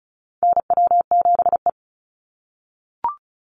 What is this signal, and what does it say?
Morse code, heard as a keyed tone, about 34 wpm, 700 Hz. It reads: NW7E
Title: NA SOTA Chaser Callsigns 35wpm